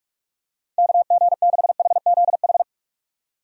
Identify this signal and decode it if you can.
Morse code, heard as a keyed tone, about 43 wpm, 700 Hz. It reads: KG6HZH
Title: USA Call Signs - 6 Characters - 45wpm